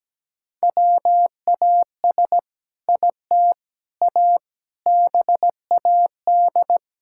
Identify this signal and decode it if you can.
Morse code, heard as a keyed tone, about 17 wpm, 700 Hz. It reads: WAS IT A BAD
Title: Sentences from Top 1000 Words - Review Part 5 - 17wpm